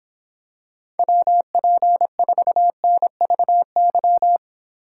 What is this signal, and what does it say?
Morse code, heard as a keyed tone, about 26 wpm, 700 Hz. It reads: WP4NVY